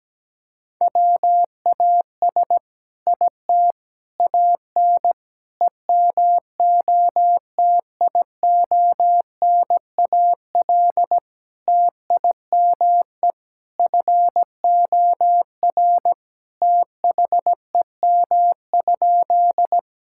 Morse code, heard as a keyed tone, about 17 wpm, 700 Hz. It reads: WAS IT AN EMOTIONAL TIME FOR THEM?
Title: Sentences from Top 1000 Words - Review Part 5 - 17wpm